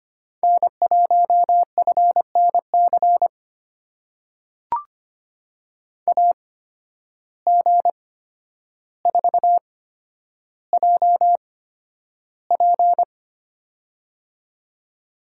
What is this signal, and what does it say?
Morse code, heard as a keyed tone, about 25 wpm, spaced as for 11 wpm, 700 Hz. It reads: N1FNC AG4JP
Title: USA Call Signs - 5 Characters - 8x Character Spacing - 25wpm